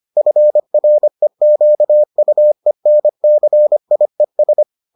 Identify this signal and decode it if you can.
Morse code, heard as a keyed tone, about 25 wpm, 600 Hz. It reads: FREQUENCIES